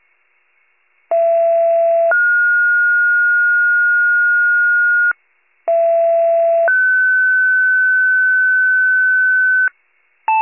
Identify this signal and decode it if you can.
Two-tone page, heard as a one-second tone, then a three-second tone, about 670 then 1430 Hz, about 670 then 1500 Hz. It is heard from 1.1 to 9.7 s.
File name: pager / 39 RESCUE DUTY / wcalert